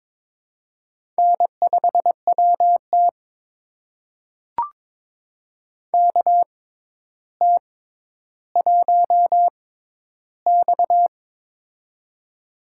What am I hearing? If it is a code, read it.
Morse code, heard as a keyed tone, about 22 wpm, spaced as for 11 wpm, 700 Hz. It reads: N5WT KT1X
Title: USA Call Signs - 4 Characters - 6x Character Spacing - 22wpm